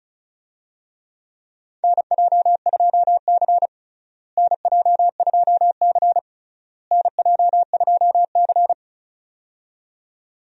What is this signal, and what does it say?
Morse code, heard as a keyed tone, about 36 wpm, 700 Hz. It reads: NJ2C NJ2C NJ2C
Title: USA Call Signs - 4 Characters - Repeated 3x - 35wpm